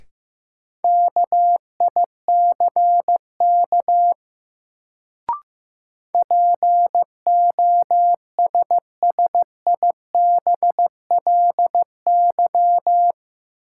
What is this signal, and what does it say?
Morse code, heard as a keyed tone, about 15 wpm, 700 Hz. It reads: KICK POSSIBLY
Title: Sentences from Top 1300 Words - New Word Drill Alt - 15wpm